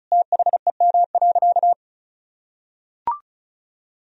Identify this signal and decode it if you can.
Morse code, heard as a keyed tone, about 36 wpm, 700 Hz. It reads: THEM.
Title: Sentences from Top 1000 Words - Review Part 1 - 35wpm